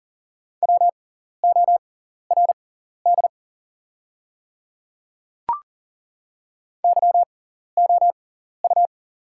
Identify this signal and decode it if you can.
Morse code, heard as a keyed tone, about 40 wpm, spaced as for 14 wpm, 690 Hz. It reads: WORD YOU
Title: 6x Character Spacing - Top 100 Words - 40wpm